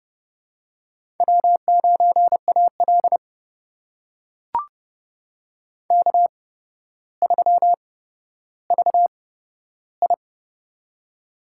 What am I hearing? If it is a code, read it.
Morse code, heard as a keyed tone, about 30 wpm, spaced as for 13 wpm, 700 Hz. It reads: W9AL K3VI
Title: USA Call Signs - 4 Characters - 8x Character Spacing - 30wpm